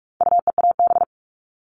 Morse code, heard as a keyed tone, about 44 wpm, 700 Hz. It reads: UEAB
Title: All Cut Numbers - 4-Digit 45wpm